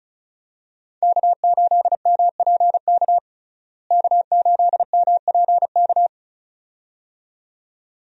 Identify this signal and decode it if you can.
Morse code, heard as a keyed tone, about 36 wpm, 700 Hz. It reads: K8MPK K8MPK
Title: USA Call Signs - 5 Characters - Repeated 2x - 35wpm